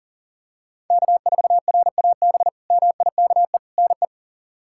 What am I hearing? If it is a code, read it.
Morse code, heard as a keyed tone, about 37 wpm, 700 Hz. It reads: K4RAB MIKE NE
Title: K1USN SST QSO Practice - 40wpm